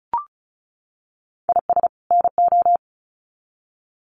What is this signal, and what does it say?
Morse code, heard as a keyed tone, about 34 wpm, 700 Hz. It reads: IS NO
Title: Sets of 2 Words 35wpm